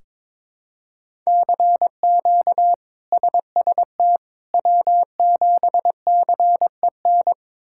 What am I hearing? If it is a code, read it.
Morse code, heard as a keyed tone, about 21 wpm, 700 Hz. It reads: CQ SST W7CEN
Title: K1USN SST QSO Practice - 22wpm